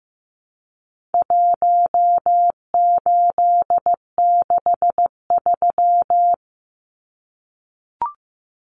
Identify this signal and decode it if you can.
Morse code, heard as a keyed tone, about 15 wpm, 700 Hz. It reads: 1863
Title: Numbers - Contest Style - 4-Digit 15wpm